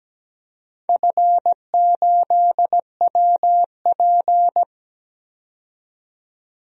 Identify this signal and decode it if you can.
Morse code, heard as a keyed tone, about 17 wpm, 700 Hz. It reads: F8WP
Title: DXCC-100 Call Signs - 4 Characters - 17wpm